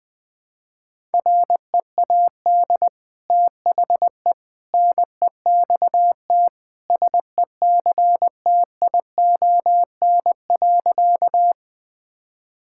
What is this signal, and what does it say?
Morse code, heard as a keyed tone, about 20 wpm, 700 Hz. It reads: READ THE NEXT SECTION.
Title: Sentences from Top 1000 Words - Review Part 5 - 20wpm